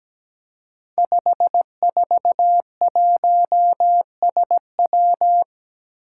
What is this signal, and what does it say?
Morse code, heard as a keyed tone, about 17 wpm, 700 Hz. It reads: 541SW
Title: Five Challenging Letter-Number Combinations 17wpm